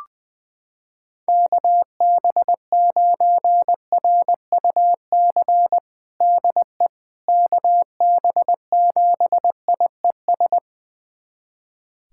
Morse code, heard as a keyed tone, about 21 wpm, 700 Hz. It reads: KB9RUC DE KB7IES
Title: USA Callsign DE Callsign - 6 Characters - 20wpm